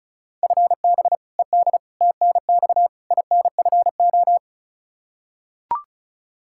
Morse code, heard as a keyed tone, about 34 wpm, 700 Hz. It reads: FB ED TNX INFO
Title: Practice QSO - Exchange 3 - More Info - 35wpm